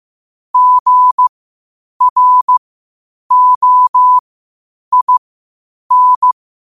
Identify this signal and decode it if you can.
Morse code, heard as a keyed tone, about 15 wpm, spaced as for 8 wpm, 1000 Hz. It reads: GROIN
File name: qotdfiles / 8wpm